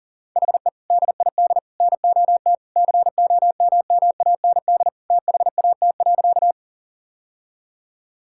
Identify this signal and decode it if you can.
Morse code, heard as a keyed tone, about 37 wpm, 700 Hz. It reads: HE DID NOT COMMAND THAT.
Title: Sentences from Top 1600 Words 40wpm